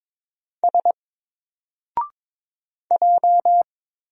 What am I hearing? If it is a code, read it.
Morse code, heard as a keyed tone, about 22 wpm, 700 Hz. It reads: S J
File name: Single-Letter-Number-22wpm